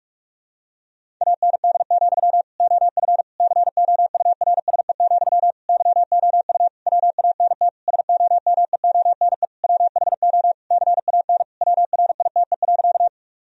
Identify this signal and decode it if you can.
Morse code, heard as a keyed tone, about 46 wpm, 700 Hz. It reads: AND, OF COURSE, YOU WANT SOMEONE WHO CAN WRITE.
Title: Sentences from Top 1000 Words - Review Part 1 - 45wpm